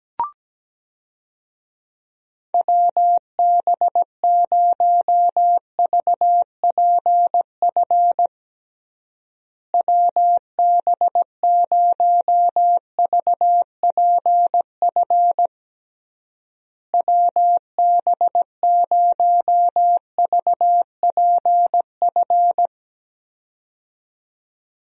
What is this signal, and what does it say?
Morse code, heard as a keyed tone, about 17 wpm, 700 Hz. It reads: WB0VPF WB0VPF WB0VPF
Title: USA Call Signs - 6 Characters - Repeated 3x - 17wpm